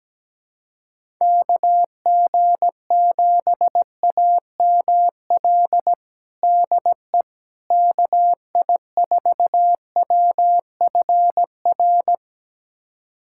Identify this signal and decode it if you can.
Morse code, heard as a keyed tone, about 17 wpm, 700 Hz. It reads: KG7AML DE KI4WFR
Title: USA Callsign DE Callsign 17wpm